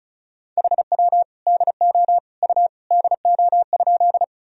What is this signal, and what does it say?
Morse code, heard as a keyed tone, about 36 wpm, 700 Hz. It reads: HW DO U DO?
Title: Sentences from Top 100 Words and Abbreviations 35wpm